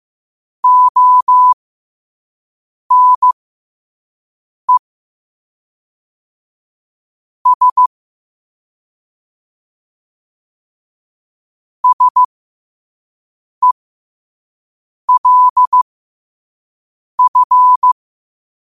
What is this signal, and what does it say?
Morse code, heard as a keyed tone, about 15 wpm, spaced as for 5 wpm, 1000 Hz. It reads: ONE S SELF
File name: qotdfiles / 5wpm